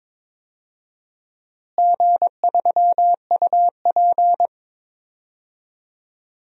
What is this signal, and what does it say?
Morse code, heard as a keyed tone, about 22 wpm, 700 Hz. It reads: G3UP
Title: DXCC-100 Call Signs - 4 Characters - Repeated 2x - 22wpm